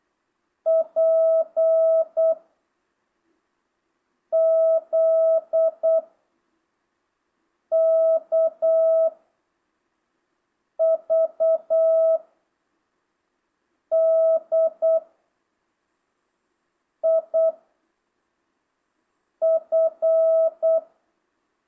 Morse code, heard as a keyed tone, about 8 wpm, 650 Hz. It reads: P Z K V D I F
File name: jh8cbh / 2025cw